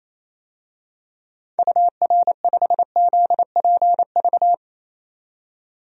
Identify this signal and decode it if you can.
Morse code, heard as a keyed tone, about 28 wpm, 700 Hz. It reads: UR5ZPV
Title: DXCC-100 Call Signs - 6 Characters - 28wpm